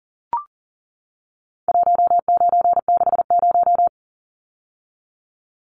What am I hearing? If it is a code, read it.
Morse code, heard as a keyed tone, about 40 wpm, 700 Hz. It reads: 1960
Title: Numbers - 4-Digit 40wpm